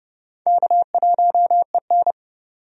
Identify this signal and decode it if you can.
Morse code, heard as a keyed tone, about 32 wpm, 700 Hz. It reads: K1EN
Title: USA Call Signs - 4 Characters - 30wpm